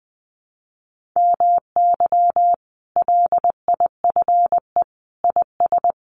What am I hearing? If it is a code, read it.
Morse code, heard as a keyed tone, about 20 wpm, 700 Hz. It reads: MY LIFE IS
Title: Sentences from Top 1400 Words 20wpm